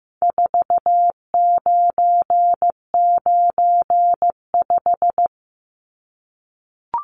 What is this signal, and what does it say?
Morse code, heard as a keyed tone, about 15 wpm, 700 Hz. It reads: 4995
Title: Numbers - 4-Digit 15wpm